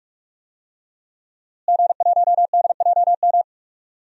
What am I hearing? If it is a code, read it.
Morse code, heard as a keyed tone, about 46 wpm, 700 Hz. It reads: G1DJM
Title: DXCC-100 Call Signs - 5 Characters - Repeated 2x - 45wpm